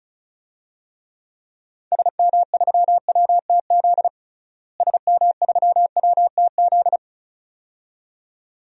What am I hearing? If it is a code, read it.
Morse code, heard as a keyed tone, about 36 wpm, 700 Hz. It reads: SM3WTZ SM3WTZ
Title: DXCC-100 Call Signs - 6 Characters - Repeated 2x - 35wpm